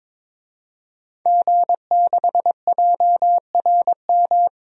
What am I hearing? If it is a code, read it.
Morse code, heard as a keyed tone, about 21 wpm, 700 Hz. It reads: G6JRM
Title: DXCC-100 Call Signs - 5 Characters - 22wpm